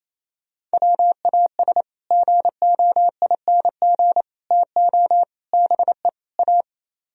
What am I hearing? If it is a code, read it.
Morse code, heard as a keyed tone, about 35 wpm, 700 Hz. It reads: WAS GOING TO BE A
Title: Speed-Racing - Sets of 5 Words 28wpm